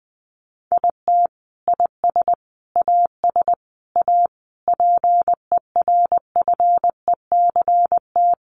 Morse code, heard as a keyed tone, about 19 wpm, 700 Hz. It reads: IT IS AS A PERFECT
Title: Sentences from Top 1000 Words 20wpm